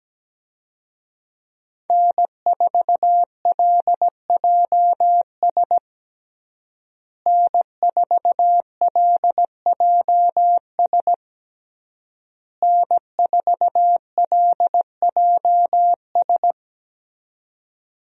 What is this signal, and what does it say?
Morse code, heard as a keyed tone, about 17 wpm, 700 Hz. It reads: N4LJS N4LJS N4LJS
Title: USA Call Signs - 5 Characters - Repeated 3x - 17wpm